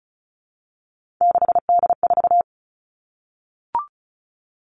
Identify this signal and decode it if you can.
Morse code, heard as a keyed tone, about 36 wpm, 700 Hz. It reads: 6D4
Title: Three Challenging Letter-Number Combinations 35wpm